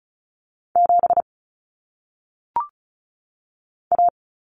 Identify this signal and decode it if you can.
Morse code, heard as a keyed tone, about 36 wpm, 700 Hz. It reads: 7 A
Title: Common Cut Numbers - 1-Digit 35wpm